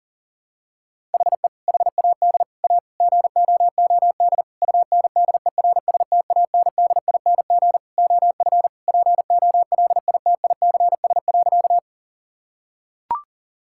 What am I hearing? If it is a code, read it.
Morse code, heard as a keyed tone, about 40 wpm, 700 Hz. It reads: HE HAD A GOOD UNDERSTANDING OF POLITICS.
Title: Sentences from Top 1000 Words - Review Part 6 - 40wpm